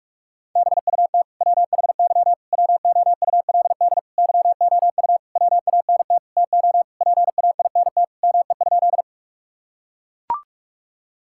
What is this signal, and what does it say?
Morse code, heard as a keyed tone, about 41 wpm, 700 Hz. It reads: BUT WHY WOULD YOU WANT TO PAINT ME?
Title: Sentences from Top 1000 Words - Review Part 6 - 45wpm